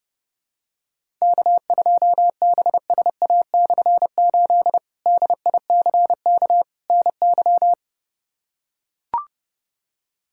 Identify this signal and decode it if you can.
Morse code, heard as a keyed tone, about 30 wpm, 700 Hz. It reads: K2BSA/8 DICK NY
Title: K1USN SST QSO Practice - 30wpm